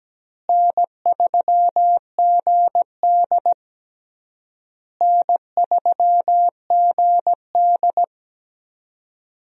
Morse code, heard as a keyed tone, about 17 wpm, 700 Hz. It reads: N3GD N3GD
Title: USA Call Signs - 4 Characters - Repeated 3x - 17wpm